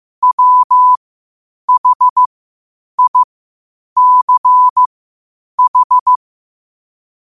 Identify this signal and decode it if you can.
Morse code, heard as a keyed tone, about 15 wpm, spaced as for 8 wpm, 1000 Hz. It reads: WHICH
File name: qotdfiles / 8wpm